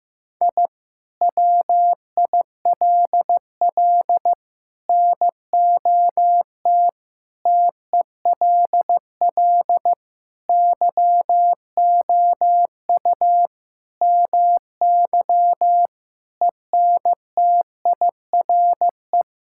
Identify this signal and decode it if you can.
Morse code, heard as a keyed tone, about 15 wpm, 700 Hz. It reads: I WILL NOT TELL YOU MY ENTIRE
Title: Sentences from Top 1000 Words - Review Part 5 - 15wpm